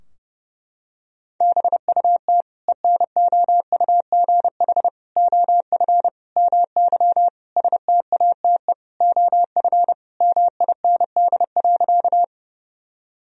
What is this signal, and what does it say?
Morse code, heard as a keyed tone, about 30 wpm, 700 Hz. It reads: BUT ENOUGH OF MY STATE OF MIND.
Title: Sentences from Top 1000 Words - Review Part 2 - 30wpm